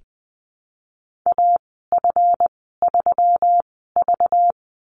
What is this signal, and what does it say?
Morse code, heard as a keyed tone, about 20 wpm, spaced as for 15 wpm, 700 Hz. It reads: AF3V
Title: 193 Lesson 31g - Letter v - Call Signs 20wpm